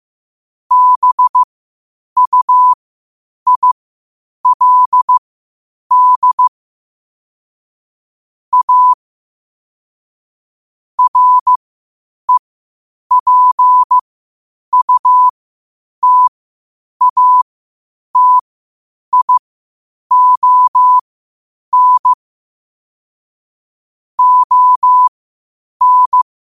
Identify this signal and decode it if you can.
Morse code, heard as a keyed tone, about 15 wpm, spaced as for 8 wpm, 1000 Hz. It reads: BUILD A REPUTATION ON